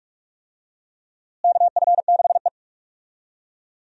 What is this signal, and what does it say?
Morse code, heard as a keyed tone, about 42 wpm, 700 Hz. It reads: KF6E
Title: USA Call Signs - 4 Characters - Repeated 2x - 45wpm